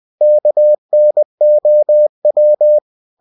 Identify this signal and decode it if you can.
Morse code, heard as a keyed tone, about 20 wpm, 600 Hz. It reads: KNOW
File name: cw-books / winnie-the-pooh / winnie-20wpm-page-0082